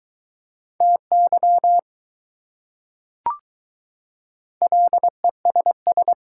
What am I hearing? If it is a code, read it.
Morse code, heard as a keyed tone, about 21 wpm, 700 Hz. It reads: TY LESS